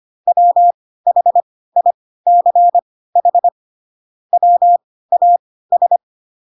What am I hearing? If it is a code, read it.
Morse code, heard as a keyed tone, about 25 wpm, spaced as for 16 wpm, 700 Hz. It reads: WHICH WAS
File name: Pinocchio / Pinocchio ch3 10 25 ik1hge